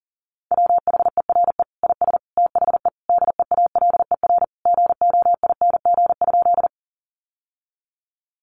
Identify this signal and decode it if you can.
Morse code, heard as a keyed tone, about 40 wpm, 700 Hz. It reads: WHERE IS THE DEALER GOING?
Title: Sentences from Top 1100 Words 40wpm